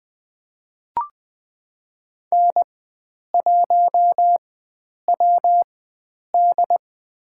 Morse code, heard as a keyed tone, about 20 wpm, spaced as for 9 wpm, 700 Hz. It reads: N1WD
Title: USA Call Signs - 4 Characters - 4x Character Spacing - 20wpm